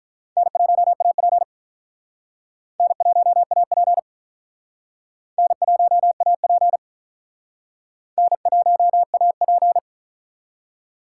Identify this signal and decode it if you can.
Morse code, heard as a keyed tone, about 48 wpm, 700 Hz. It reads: N1AP N1AP N1AP N1AP